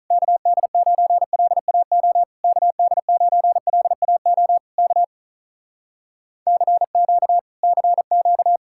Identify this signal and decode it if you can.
Morse code, heard as a keyed tone, about 46 wpm, 700 Hz. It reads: KD9LAO KD9LAO K CQ CQ